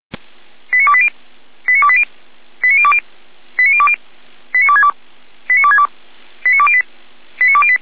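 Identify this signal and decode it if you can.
Five-tone selective call (ZVEI), heard as quick runs of five tones, tones of about 70 ms. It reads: 89280 89280 89020 89020 89262 89262 89208 89208
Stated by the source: ZVEI five-tone selective call received by SDR, logged by the fire brigade server feuerwehr-orlamuende.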